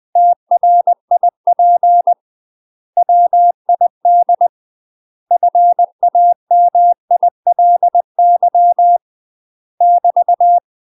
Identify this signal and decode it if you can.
Morse code, heard as a keyed tone, about 20 wpm, 700 Hz. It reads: TRIP WID FAMILY <BT>
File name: UK MASTER QSOs TRAINING/20WPM